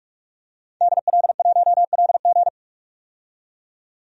Morse code, heard as a keyed tone, about 46 wpm, 700 Hz. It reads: DL1LG
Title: DXCC-100 Call Signs - 5 Characters - 45wpm